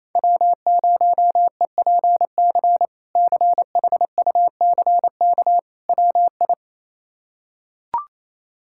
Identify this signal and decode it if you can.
Morse code, heard as a keyed tone, about 28 wpm, 700 Hz. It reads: W0EPC CHUCK WI
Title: K1USN SST QSO Practice - 28wpm